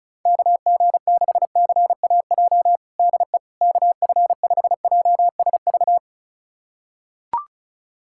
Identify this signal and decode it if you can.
Morse code, heard as a keyed tone, about 36 wpm, 700 Hz. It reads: KG6CAJ DE KF5JSV
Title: USA Callsign DE Callsign - 6 Characters - 35wpm